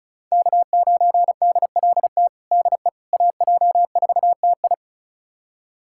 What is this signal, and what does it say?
Morse code, heard as a keyed tone, about 36 wpm, 700 Hz. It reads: K9DLT DE AJ4TI
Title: USA Callsign DE Callsign - 5 Characters - 35wpm